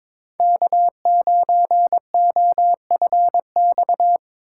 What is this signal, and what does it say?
Morse code, heard as a keyed tone, about 25 wpm, 700 Hz. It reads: K9OFX
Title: Speed Racing - USA 1x3 Call Signs 22wpm